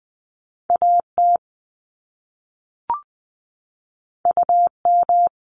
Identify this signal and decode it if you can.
Morse code, heard as a keyed tone, about 20 wpm, 700 Hz. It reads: A T U M
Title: Top Two Letter Combinations in Words 20wpm